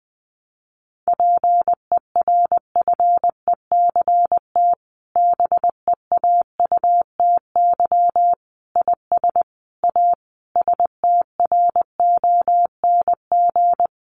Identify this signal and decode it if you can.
Morse code, heard as a keyed tone, about 20 wpm, 700 Hz. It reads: PERFECT BEAUTY IS A STRONG
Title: Sentences from Top 1200 Words 20wpm